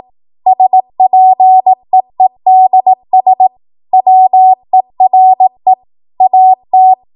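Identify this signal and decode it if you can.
Morse code, heard as a keyed tone, about 18 wpm, 750 Hz. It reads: SPEEDS WERE AT
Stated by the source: ARRL 18 wpm Morse code practice archive transmission.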